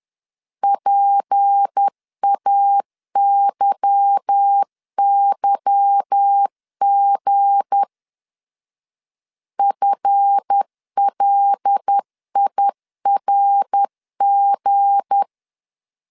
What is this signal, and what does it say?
Morse code, heard as a keyed tone, about 10 wpm, 800 Hz. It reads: PAYYG FLIRG
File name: LEZIONI / 8 YJ45 / 8 Y 10wpm 2615